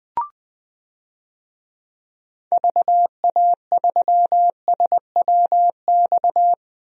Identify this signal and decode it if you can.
Morse code, heard as a keyed tone, about 21 wpm, 700 Hz. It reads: VA3SWX
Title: DXCC-100 Call Signs - 6 Characters - 20wpm